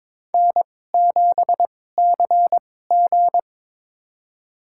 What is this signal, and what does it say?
Morse code, heard as a keyed tone, about 22 wpm, spaced as for 16 wpm, 700 Hz. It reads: N7CG
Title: USA Call Signs - 4 Characters - 2x Character Spacing - 22wpm